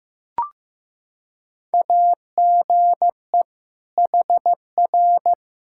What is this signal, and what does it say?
Morse code, heard as a keyed tone, about 15 wpm, 700 Hz. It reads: AGE HR
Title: K1USN SST QSO Practice 15wpm